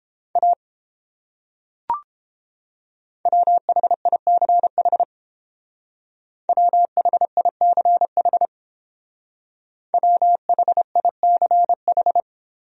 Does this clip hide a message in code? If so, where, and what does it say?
Yes: Morse code from 0.4 to 12.2 s, heard as a keyed tone, about 30 wpm, 700 Hz. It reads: A WHICH WHICH WHICH